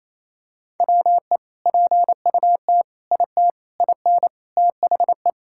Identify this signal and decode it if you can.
Morse code, heard as a keyed tone, about 28 wpm, 700 Hz. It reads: WE PUT IT IN THE
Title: Sentences from Top 1000 Words - Review Part 5 - 28wpm